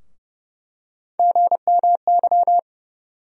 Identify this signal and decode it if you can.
Morse code, heard as a keyed tone, about 30 wpm, 700 Hz. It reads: GMY
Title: USA Call Sign Postfixes - 3 Letters 30wpm